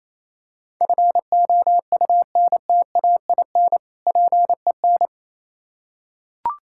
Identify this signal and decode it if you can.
Morse code, heard as a keyed tone, about 27 wpm, 690 Hz. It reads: FOUNTAIN PEN
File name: Easy Words - Speed-Racing with Hints - 22wpm